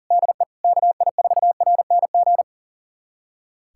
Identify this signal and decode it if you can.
Morse code, heard as a keyed tone, about 37 wpm, 700 Hz. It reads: DE KI4RNG
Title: USA Callsign DE Callsign 40wpm